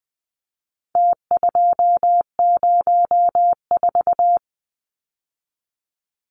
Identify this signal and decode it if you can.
Morse code, heard as a keyed tone, about 20 wpm, 700 Hz. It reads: T204
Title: Numbers - Contest Style - 4-Digit 20wpm